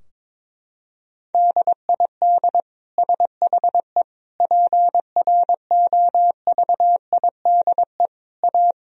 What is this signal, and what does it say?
Morse code, heard as a keyed tone, about 22 wpm, 700 Hz. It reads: DID SHE PROVIDE A
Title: Sentences from Top 1000 Words - Review Part 4 - 22wpm